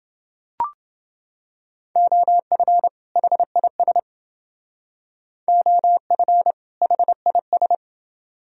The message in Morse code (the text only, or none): OF HIS OF HIS